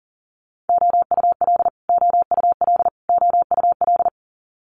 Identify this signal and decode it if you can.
Morse code, heard as a keyed tone, about 40 wpm, 700 Hz. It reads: OUL OUL OUL
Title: Multi-letter Phonemes Repeated 3x 40wpm